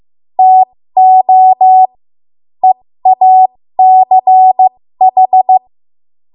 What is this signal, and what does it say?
Morse code, heard as a keyed tone, about 15 wpm, spaced as for 13 wpm, 750 Hz. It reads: TO EACH
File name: Morse / 13WPM